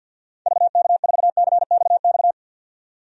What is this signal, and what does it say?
Morse code, heard as a keyed tone, about 52 wpm, 700 Hz. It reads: VK4CXX
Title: DXCC-100 Call Signs - 6 Characters - 50wpm